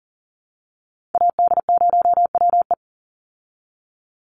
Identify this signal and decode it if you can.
Morse code, heard as a keyed tone, about 40 wpm, 700 Hz. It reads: AD0WE
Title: Call Signs Repeated 2x - 40wpm